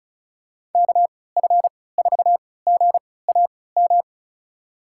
Morse code, heard as a keyed tone, about 36 wpm, spaced as for 20 wpm, 700 Hz. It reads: KF4GAM
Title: USA Call Signs - 6 Characters - 3x Character Spacing - 35wpm